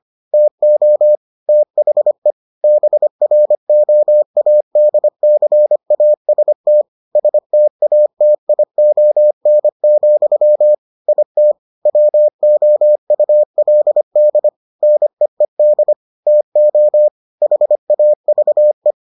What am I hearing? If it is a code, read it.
Morse code, heard as a keyed tone, about 25 wpm, 600 Hz. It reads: TO THE BROADCAST STATION, IT WOULD NEED TO HAVE